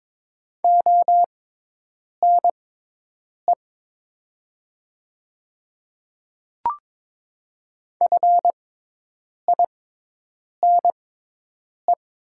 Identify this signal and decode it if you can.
Morse code, heard as a keyed tone, about 22 wpm, spaced as for 8 wpm, 700 Hz. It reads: ONE FINE
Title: 6x Character Spacing - Top 500 Words - 22wpm